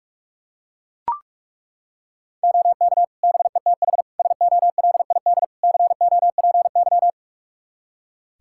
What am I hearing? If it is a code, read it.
Morse code, heard as a keyed tone, about 41 wpm, 700 Hz. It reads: OK BETH SOLID COPY
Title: Practice QSO - Exchange 3 - More Info - 45wpm